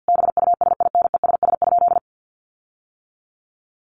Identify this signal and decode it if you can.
Morse code, heard as a keyed tone, about 50 wpm, 700 Hz. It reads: BUSINESS?
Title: Sentences from Top 1400 Words 50wpm